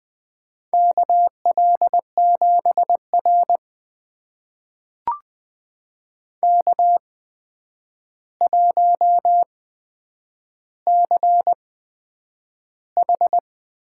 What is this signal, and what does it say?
Morse code, heard as a keyed tone, about 20 wpm, spaced as for 9 wpm, 700 Hz. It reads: KL7R K1CH
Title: USA Call Signs - 4 Characters - 8x Character Spacing - 20wpm